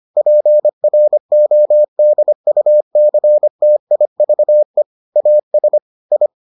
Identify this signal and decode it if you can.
Morse code, heard as a keyed tone, about 25 wpm, 600 Hz. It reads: PRODUCTIVE AS I